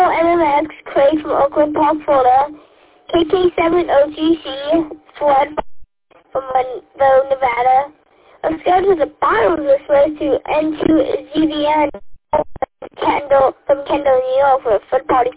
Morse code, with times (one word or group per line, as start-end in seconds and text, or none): none